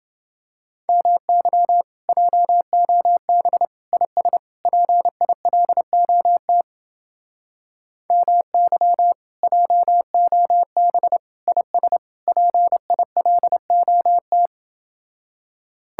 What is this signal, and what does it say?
Morse code, heard as a keyed tone, about 28 wpm, 700 Hz. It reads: MY JOB IS PILOT MY JOB IS PILOT